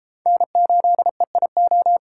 Morse code, heard as a keyed tone, about 30 wpm, 700 Hz. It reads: N8EIO